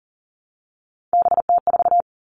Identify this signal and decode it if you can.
Morse code, heard as a keyed tone, about 38 wpm, 700 Hz. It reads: BT4
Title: All Cut Numbers - 3-Digit 40wpm